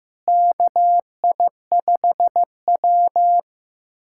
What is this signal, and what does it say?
Morse code, heard as a keyed tone, about 15 wpm, 700 Hz. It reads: KI5W
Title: USA Callsign DE Callsign - 4 Characters - 15wpm